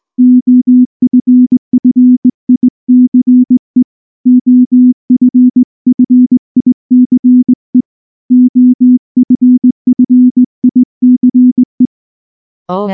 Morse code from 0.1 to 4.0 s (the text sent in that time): OFFICE